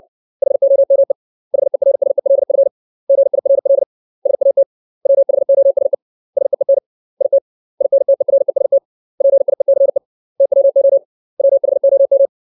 Morse code, heard as a keyed tone, about 60 wpm, 550 Hz. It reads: HOME HERSELF GIRL SAT WHOSE SEEN IT INTEREST PIECE TWO WHOM
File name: 632 words / 632 Words 60WPM